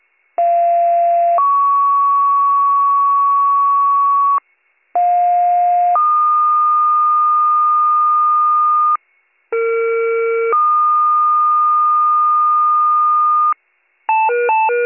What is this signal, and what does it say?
Two-tone page, heard as a one-second tone, then a three-second tone, about 690 then 1120 Hz, about 710 then 1220 Hz, about 460 then 1220 Hz. It is heard from 0.4 to 13.5 s.